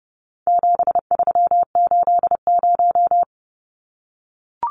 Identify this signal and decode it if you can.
Morse code, heard as a keyed tone, about 30 wpm, 700 Hz. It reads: 7380
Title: Numbers - 4-Digit 30wpm